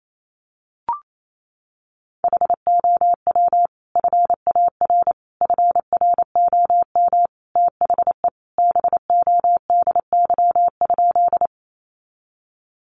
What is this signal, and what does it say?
Morse code, heard as a keyed tone, about 28 wpm, 700 Hz. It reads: HOW FAR FROM THE BODY?
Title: Sentences from Top 400 Words 28wpm